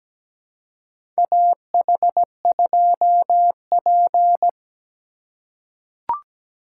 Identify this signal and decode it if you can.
Morse code, heard as a keyed tone, about 17 wpm, 700 Hz. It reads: AH2P
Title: USA Call Signs - 3x Character Spacing - 17wpm